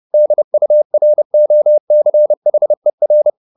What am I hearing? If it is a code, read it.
Morse code, heard as a keyed tone, about 30 wpm, 600 Hz. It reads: DUROCHER
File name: the old man and the sea / the old man and the sea 1 30wpm